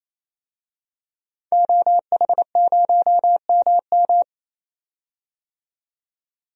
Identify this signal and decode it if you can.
Morse code, heard as a keyed tone, about 28 wpm, 700 Hz. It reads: OH0MM
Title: DXCC-100 Call Signs - 5 Characters - 28wpm